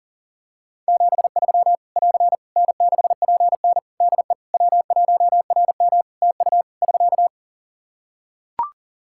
Morse code, heard as a keyed tone, about 40 wpm, 700 Hz. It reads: 73 <AR> N6PN DE W1RM TU <SK>
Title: Practice QSO - Exchange 4 - Ending - 40wpm